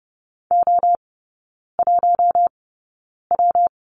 Morse code, heard as a keyed tone, about 30 wpm, spaced as for 9 wpm, 700 Hz. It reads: O1W
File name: Lesson 19e - Letter w - Three Characters 30wpm